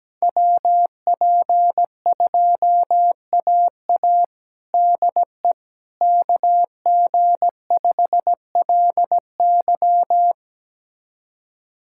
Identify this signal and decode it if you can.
Morse code, heard as a keyed tone, about 17 wpm, 700 Hz. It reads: WP2AA DE KG5LY
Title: USA Callsign DE Callsign - 5 Characters - 17wpm